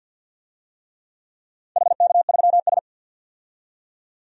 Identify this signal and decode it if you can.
Morse code, heard as a keyed tone, about 48 wpm, 700 Hz. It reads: HK3S
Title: DXCC-100 Call Signs - 4 Characters - 50wpm